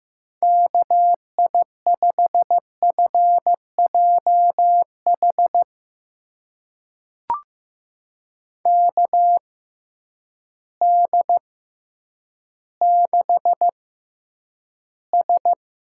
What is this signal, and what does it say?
Morse code, heard as a keyed tone, about 15 wpm, spaced as for 9 wpm, 700 Hz. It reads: KI5FJH KD6S
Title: USA Call Signs - 6 Characters - 6x Character Spacing - 15wpm